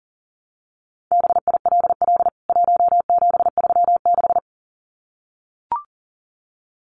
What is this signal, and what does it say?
Morse code, heard as a keyed tone, about 40 wpm, 700 Hz. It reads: BILL 1736
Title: CWT Exchange 40wpm